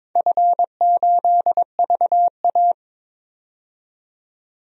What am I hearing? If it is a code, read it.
Morse code, heard as a keyed tone, about 22 wpm, 700 Hz. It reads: F8VA
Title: DXCC-100 Call Signs - 4 Characters - 22wpm